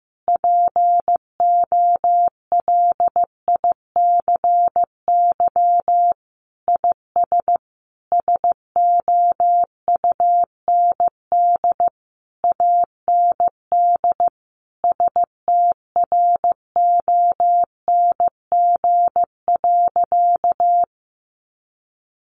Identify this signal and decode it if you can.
Morse code, heard as a keyed tone, about 14 wpm, 700 Hz. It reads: POLICY IS SOUND AND STRONG.
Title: Sentences from Top 1400 Words 15wpm